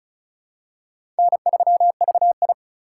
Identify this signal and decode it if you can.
Morse code, heard as a keyed tone, about 34 wpm, 700 Hz. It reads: N3VI
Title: USA Call Signs - 4 Characters - 4x Character Spacing - 35wpm